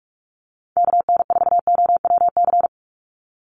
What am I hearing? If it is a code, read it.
Morse code, heard as a keyed tone, about 46 wpm, 690 Hz. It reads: KN4OWC
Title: USA Call Signs 45wpm